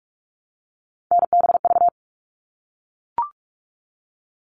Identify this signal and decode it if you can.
Morse code, heard as a keyed tone, about 44 wpm, 700 Hz. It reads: NBV
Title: All Cut Numbers - 3-Digit 45wpm